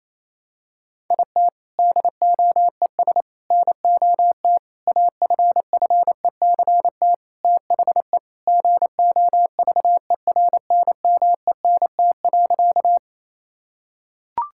Morse code, heard as a keyed tone, about 28 wpm, 700 Hz. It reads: IT DOES NOT AFFECT THE GOVERNMENT.
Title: Sentences from Top 1000 Words - Review Part 6 - 28wpm